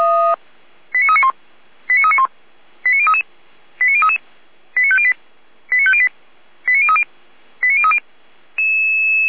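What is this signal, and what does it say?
Five-tone selective call (ZVEI), heard as quick runs of five tones, tones of about 70 ms. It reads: 89391 89391 89033 89033 89508 89508 89030 89030
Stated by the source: SDR-received ZVEI five-tone selective call, code 89050.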